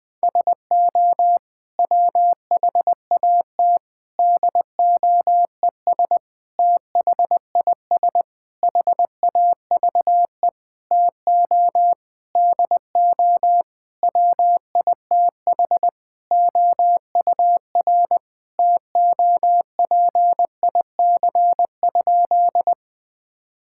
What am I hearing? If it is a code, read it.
Morse code, heard as a keyed tone, about 20 wpm, 700 Hz. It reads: SO WHAT DOES THIS HAVE TO DO WITH OUR TOPIC?
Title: Sentences with 8-Word Phrases - 20wpm